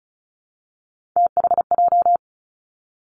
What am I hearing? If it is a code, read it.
Morse code, heard as a keyed tone, about 36 wpm, 700 Hz. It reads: THJ
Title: Three Challenging Letter-Number Combinations 35wpm